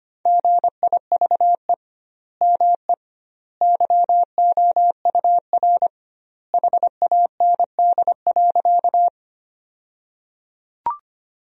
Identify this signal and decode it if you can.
Morse code, heard as a keyed tone, about 25 wpm, 700 Hz. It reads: GIVE ME YOUR HAND.
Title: 2x Word-Spacing - Sentences from Top 200 Words - 25wpm